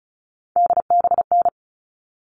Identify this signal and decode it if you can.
Morse code, heard as a keyed tone, about 34 wpm, 700 Hz. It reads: DBN
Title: All Cut Numbers - 3-Digit 35wpm